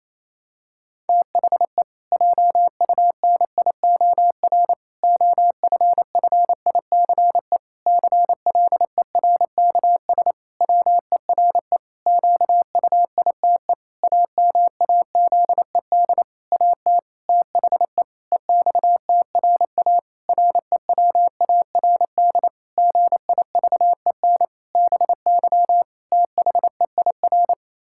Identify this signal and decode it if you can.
Morse code, heard as a keyed tone, about 28 wpm, 700 Hz. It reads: THE JUNIOR OFFICE CLERKS WERE QUITE AMAZED AT THE EXTRA REWARD GIVEN BY THEIR
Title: Pangram Challenge v2 - 28wpm